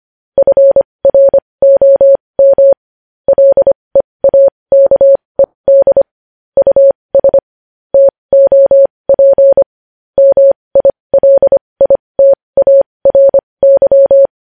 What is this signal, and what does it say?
Morse code, heard as a keyed tone, about 25 wpm, spaced as for 20 wpm, 550 Hz. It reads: FROM LEAKED US TOP MILITARY
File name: podcast / News headlines-1x-25x20